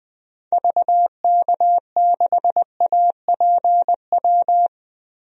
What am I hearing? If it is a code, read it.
Morse code, heard as a keyed tone, about 20 wpm, 700 Hz. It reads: VK6APW
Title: DXCC-100 Call Signs - 6 Characters - Repeated 3x - 20wpm